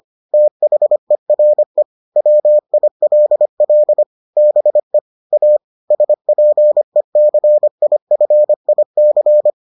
Morse code, heard as a keyed tone, about 25 wpm, 600 Hz. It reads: THERE WILL BE A SPECIFIC